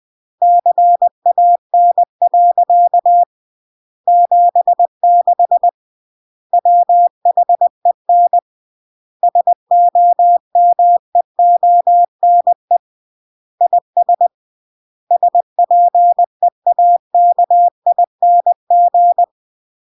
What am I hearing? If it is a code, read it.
Morse code, heard as a keyed tone, about 20 wpm, 700 Hz. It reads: CAN. 76 WHEN SOMEONE IS SPEAKING
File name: CW / jodorowsky / 82 commandments 20-20